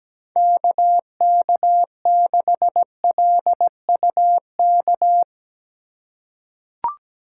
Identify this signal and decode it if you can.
Morse code, heard as a keyed tone, about 17 wpm, 700 Hz. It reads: KK6LUK
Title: USA Call Signs - 6 Characters - 2x Character Spacing - 17wpm